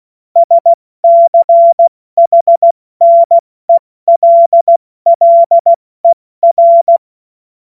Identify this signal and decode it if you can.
Morse code, heard as a keyed tone, about 16 wpm, 680 Hz. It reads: SCHNELLER
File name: Morsekurs / Witze / Witz 25. Oktober 2022 70 F80